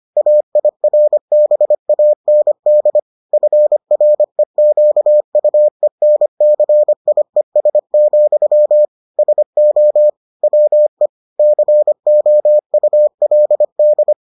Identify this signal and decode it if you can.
Morse code, heard as a keyed tone, about 25 wpm, 600 Hz. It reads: AIRBAND FREQUENCIES, SO WE COULD